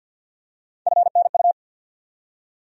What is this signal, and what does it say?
Morse code, heard as a keyed tone, about 50 wpm, 700 Hz. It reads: FNU
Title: USA Call Sign Postfixes - 3 Letters 50wpm